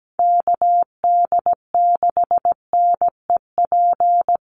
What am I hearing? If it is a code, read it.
Morse code, heard as a keyed tone, about 17 wpm, 700 Hz. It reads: KD6NEP
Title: USA Call Signs Repeated 3x 17wpm